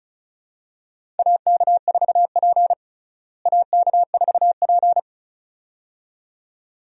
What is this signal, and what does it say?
Morse code, heard as a keyed tone, about 34 wpm, 700 Hz. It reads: AK4P AK4P
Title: USA Call Signs - 4 Characters - Repeated 2x - 35wpm